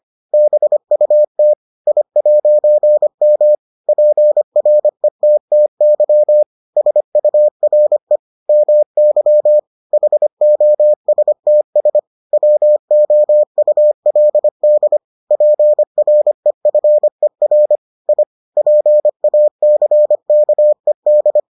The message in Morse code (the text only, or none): BUT I'M PRETTY SURE MY HOSTS WOULD PREFER I PACKED